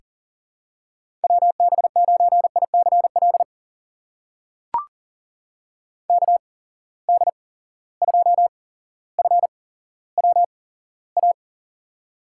Morse code, heard as a keyed tone, about 40 wpm, spaced as for 17 wpm, 700 Hz. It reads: WB9ICL KD2FWA
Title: USA Call Signs - 6 Characters - 8x Character Spacing - 40wpm